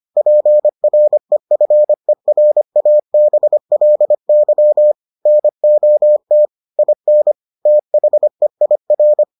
Morse code, heard as a keyed tone, about 25 wpm, 600 Hz. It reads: PREFERABLY NOT IN THEIR